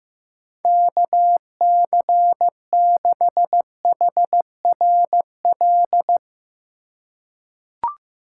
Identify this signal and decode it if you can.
Morse code, heard as a keyed tone, about 15 wpm, 700 Hz. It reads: KC6HRL
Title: USA Call Signs - 2x Character Spacing - 15wpm